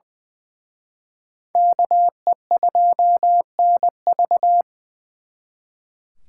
Morse code, heard as a keyed tone, about 24 wpm, 700 Hz. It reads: KE2NV
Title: Speed Racing - USA Call Signs 20wpm